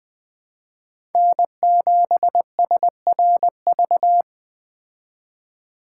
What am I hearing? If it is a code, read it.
Morse code, heard as a keyed tone, about 20 wpm, 700 Hz. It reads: N7SRV